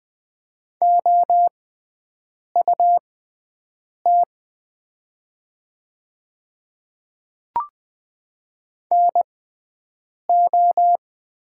Morse code, heard as a keyed tone, about 20 wpm, spaced as for 7 wpm, 700 Hz. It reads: OUT NO
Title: 6x Character Spacing - Top 100 Words - 20wpm